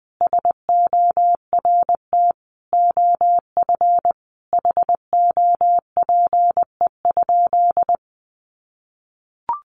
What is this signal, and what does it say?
Morse code, heard as a keyed tone, about 20 wpm, 700 Hz. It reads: SORT OF HOPE?
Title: Sentences from Top 1300 Words 20wpm